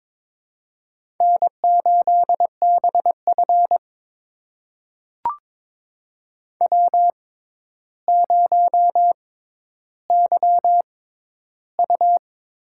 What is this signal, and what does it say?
Morse code, heard as a keyed tone, about 22 wpm, spaced as for 11 wpm, 700 Hz. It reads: N8BF W0YU
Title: USA Call Signs - 4 Characters - 6x Character Spacing - 22wpm